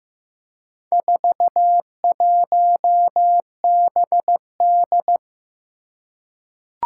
Morse code, heard as a keyed tone, about 15 wpm, 700 Hz. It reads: 41BD
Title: Four Challenging Letter-Number Combinations 15wpm